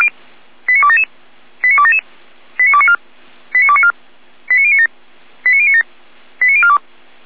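Five-tone selective call (ZVEI), heard as quick runs of five tones, tones of about 70 ms. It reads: 89277 89277 89294 89294 89097 89097 89042
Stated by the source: ZVEI five-tone selective call 89310, received by SDR.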